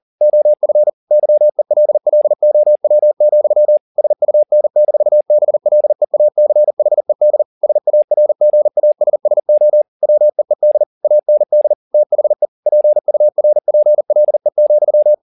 Morse code, heard as a keyed tone, about 40 wpm, 600 Hz. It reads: OF YELLOW, SUN-BLEACHED SARGASSO WEED AND THE PURPLE,